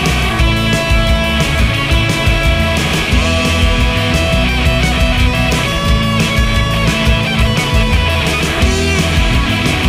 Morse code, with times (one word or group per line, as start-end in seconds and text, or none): none